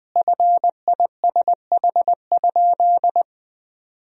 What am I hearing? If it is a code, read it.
Morse code, heard as a keyed tone, about 20 wpm, 700 Hz. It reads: FISH?
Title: Sentences from Top 1000 Words - Review Part 1 - 20wpm